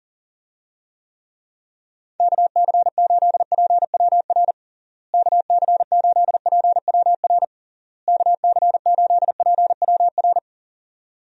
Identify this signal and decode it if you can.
Morse code, heard as a keyed tone, about 40 wpm, 700 Hz. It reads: KC8PWR KC8PWR KC8PWR
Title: USA Call Signs - 6 Characters - Repeated 3x - 40wpm